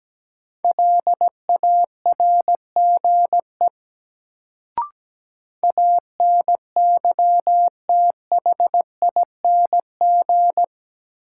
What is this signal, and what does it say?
Morse code, heard as a keyed tone, about 17 wpm, 700 Hz. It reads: LARGE ANYTHING
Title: Sets of 4 Words - New Word Drill - 17wpm